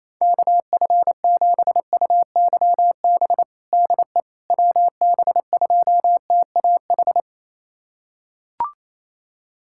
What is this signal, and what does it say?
Morse code, heard as a keyed tone, about 28 wpm, 700 Hz. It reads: KF7UYB DE WB2TAH
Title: USA Callsign DE Callsign - 6 Characters - 28wpm